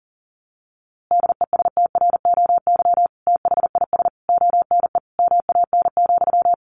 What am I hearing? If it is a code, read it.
Morse code, heard as a keyed tone, about 40 wpm, 700 Hz. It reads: DESTROY THIS ONE MAN,
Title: A Sound Of Thunder 40wpm